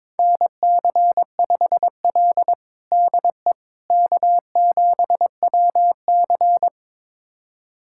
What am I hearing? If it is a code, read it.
Morse code, heard as a keyed tone, about 22 wpm, 700 Hz. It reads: NC5L DE K7WC
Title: USA Callsign DE Callsign - 4 Characters - 22wpm